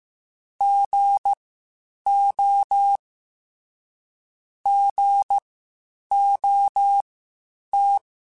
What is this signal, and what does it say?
Morse code, heard as a keyed tone, about 15 wpm, spaced as for 8 wpm, 800 Hz. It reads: GO GOT
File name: Misc / Misc-mp3s / Gotten @13